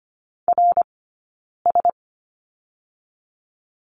Morse code, heard as a keyed tone, about 24 wpm, 700 Hz. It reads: R S
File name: course / 25:10wpm / 092 Lesson 16d - Letter c - Two Characters 25wpm